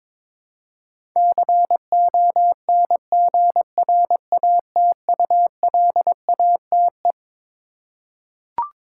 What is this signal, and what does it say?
Morse code, heard as a keyed tone, about 22 wpm, 700 Hz. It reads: CONGRATULATE
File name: Twelve Letter Words - 22wpm